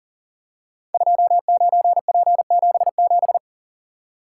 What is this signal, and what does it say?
Morse code, heard as a keyed tone, about 42 wpm, 700 Hz. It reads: 29P77
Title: Code Groups - 5 Letter:Numbers - 40wpm